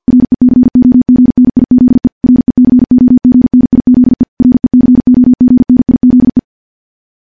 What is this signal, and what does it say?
Morse code, heard as a keyed tone, about 47 wpm, 250 Hz. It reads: RECOGNIZE RECOGNIZE RECOGNIZE
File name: morse / morse-training-50WPM